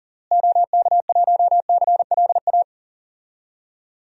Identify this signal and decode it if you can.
Morse code, heard as a keyed tone, about 40 wpm, 700 Hz. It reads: OK1CLA
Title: DXCC-100 Call Signs - 6 Characters - Repeated 3x - 40wpm